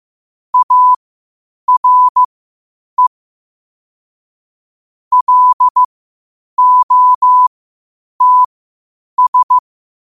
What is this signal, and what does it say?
Morse code, heard as a keyed tone, about 15 wpm, spaced as for 8 wpm, 1000 Hz. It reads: ARE LOTS
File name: qotdfiles / 8wpm